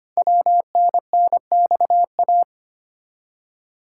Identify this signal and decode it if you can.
Morse code, heard as a keyed tone, about 25 wpm, 700 Hz. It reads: WNNXA
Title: Code Groups - 5 Letters - 25wpm